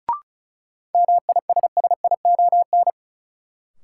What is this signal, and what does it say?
Morse code, heard as a keyed tone, about 36 wpm, 700 Hz. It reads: MISSION
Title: Sentences from Top 1600 Words - New Word Drill - 35wpm